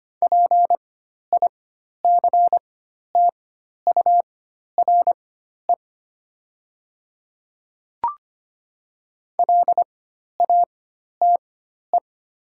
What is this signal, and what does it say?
Morse code, heard as a keyed tone, about 25 wpm, spaced as for 12 wpm, 700 Hz. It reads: PICTURE LATE
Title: 4x Character Spacing - Top 300 Words - 25wpm